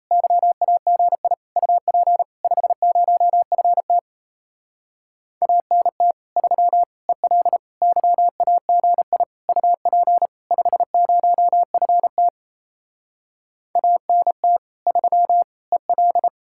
Morse code, heard as a keyed tone, about 37 wpm, 700 Hz. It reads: YAGI UP 50FT ANT 3 EL YAGI UP 50FT ANT 3 EL